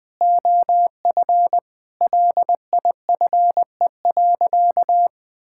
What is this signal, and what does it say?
Morse code, heard as a keyed tone, about 21 wpm, 700 Hz. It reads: OF LIFE.
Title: Sentences from Top 1000 Words - Review Part 5 - 20wpm